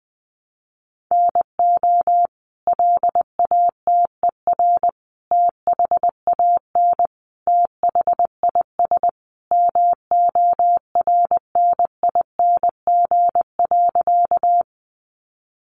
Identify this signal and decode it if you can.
Morse code, heard as a keyed tone, about 20 wpm, 700 Hz. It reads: NO LATER THAN THIS MORNING.
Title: Sentences from Top 700 Words 20wpm